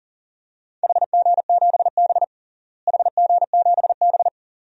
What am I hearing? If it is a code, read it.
Morse code, heard as a keyed tone, about 40 wpm, 700 Hz. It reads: HG7B HG7B
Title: DXCC-100 Call Signs - 4 Characters - Repeated 2x - 40wpm